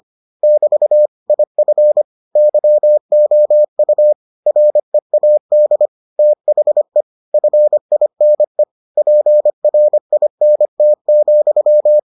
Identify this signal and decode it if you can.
Morse code, heard as a keyed tone, about 25 wpm, 600 Hz. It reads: <BT> IF YOU READ THE FINE PRINT,